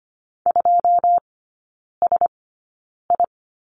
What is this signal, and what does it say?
Morse code, heard as a keyed tone, about 25 wpm, spaced as for 9 wpm, 690 Hz. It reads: 2SI
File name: Lesson 14e - Number 2 - Three Characters 25wpm